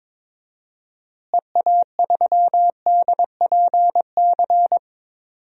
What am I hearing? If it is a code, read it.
Morse code, heard as a keyed tone, about 23 wpm, 700 Hz. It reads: EA3DPC
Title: DXCC-100 Call Signs - 6 Characters - Repeated 3x - 22wpm